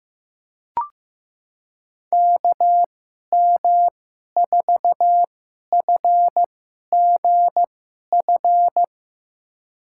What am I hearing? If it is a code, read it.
Morse code, heard as a keyed tone, about 15 wpm, spaced as for 11 wpm, 700 Hz. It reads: KM4FGF
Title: USA Call Signs - 6 Characters - 2x Character Spacing - 15wpm